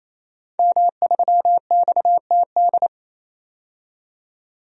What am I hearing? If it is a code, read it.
Morse code, heard as a keyed tone, about 28 wpm, 700 Hz. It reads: M3XTD
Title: DXCC-100 Call Signs - 5 Characters - 28wpm